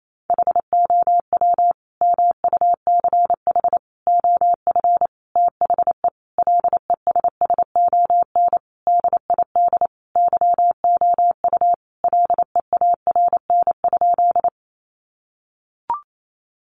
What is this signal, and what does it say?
Morse code, heard as a keyed tone, about 28 wpm, 700 Hz. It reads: HOW MUCH OF THE LESSON DID YOU LEARN?
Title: Sentences from Top 1200 Words 28wpm